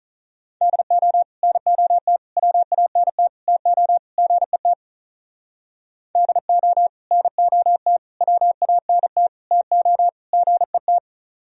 Morse code, heard as a keyed tone, about 42 wpm, 700 Hz. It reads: DO NOT WANT TO GET DO NOT WANT TO GET
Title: Speed-Racing - Sets of 5 Words 35wpm